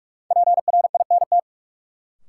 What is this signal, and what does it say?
Morse code, heard as a keyed tone, about 48 wpm, 700 Hz. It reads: PRINT
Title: Sentences from Top 1400 Words - New Word Drill - 45wpm